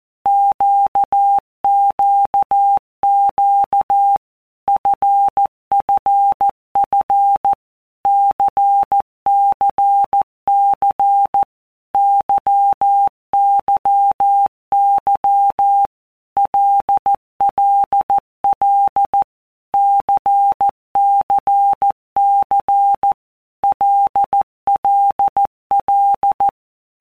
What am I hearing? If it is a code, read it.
Morse code, heard as a keyed tone, about 14 wpm, 800 Hz. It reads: QQQ FFF CCC YYY LLL CCC LLL
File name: Basic Letter Groups By WPM's / Groups@13 / LFCYQ@13